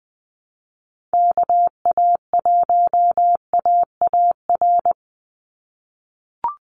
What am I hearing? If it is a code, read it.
Morse code, heard as a keyed tone, about 20 wpm, 700 Hz. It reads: KA1AAR
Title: USA Call Signs 20wpm